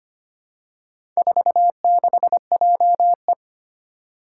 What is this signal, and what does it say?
Morse code, heard as a keyed tone, about 25 wpm, 700 Hz. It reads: 46JE
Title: Four Challenging Letter-Number Combinations 25wpm